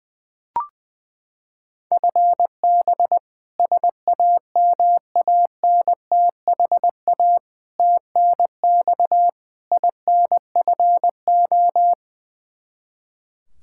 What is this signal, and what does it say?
Morse code, heard as a keyed tone, about 21 wpm, 700 Hz. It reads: FB SAMANTHA TNX INFO
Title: K1USN SST QSO Practice 20wpm